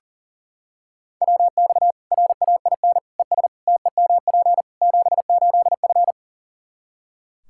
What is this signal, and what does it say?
Morse code, heard as a keyed tone, about 40 wpm, 700 Hz. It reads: WX RAIN ES TEMP 78F
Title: K1USN SST QSO Practice 40wpm